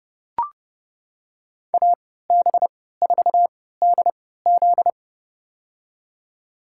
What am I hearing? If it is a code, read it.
Morse code, heard as a keyed tone, about 30 wpm, spaced as for 17 wpm, 700 Hz. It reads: AB4DZ
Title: USA Call Signs - 3x Character Spacing - 30wpm